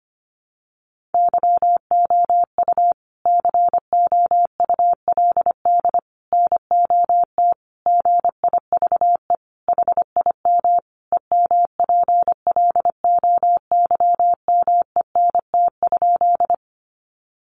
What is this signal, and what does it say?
Morse code, heard as a keyed tone, about 25 wpm, 700 Hz. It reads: YOU COULD NOT GIVE HIM EMPLOYMENT?
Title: Sentences from Top 1500 Words 25wpm